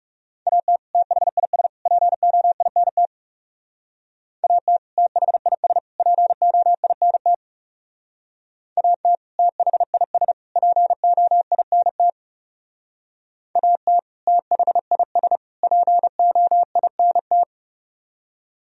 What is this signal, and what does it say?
Morse code, heard as a keyed tone, about 38 wpm, 700 Hz. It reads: AT THIS POINT AT THIS POINT AT THIS POINT AT THIS POINT